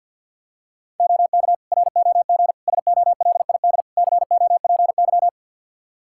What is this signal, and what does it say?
Morse code, heard as a keyed tone, about 51 wpm, 700 Hz. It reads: OK ROG SOLID COPY
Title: Practice QSO - Exchange 3 - More Info - 50wpm